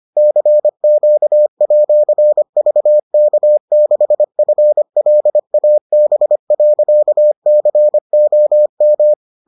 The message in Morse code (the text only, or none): CQ@VK6FLAB.COM